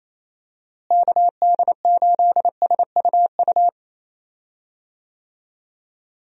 Morse code, heard as a keyed tone, about 28 wpm, 700 Hz. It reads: KD8SUU
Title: USA Call Signs - 6 Characters - Repeated 2x - 28wpm